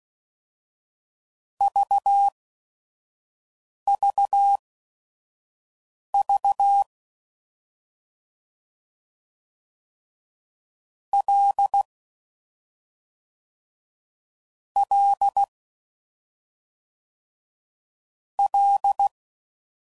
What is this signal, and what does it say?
Morse code, heard as a keyed tone, about 16 wpm, spaced as for 5 wpm, 800 Hz. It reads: VVV L L L